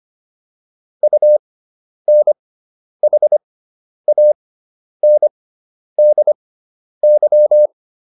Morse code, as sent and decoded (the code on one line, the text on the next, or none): ..- -. .... .- -. -.. -.--
UNHANDY